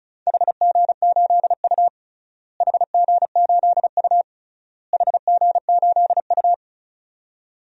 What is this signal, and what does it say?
Morse code, heard as a keyed tone, about 34 wpm, 700 Hz. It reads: HG8U HG8U HG8U
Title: DXCC-100 Call Signs - 4 Characters - Repeated 3x - 35wpm